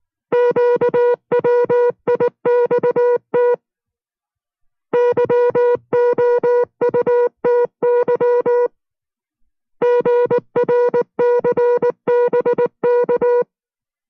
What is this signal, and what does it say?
Morse code, heard as a keyed tone, about 19 wpm, 480 Hz. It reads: QWIXT YOUTY GRCBK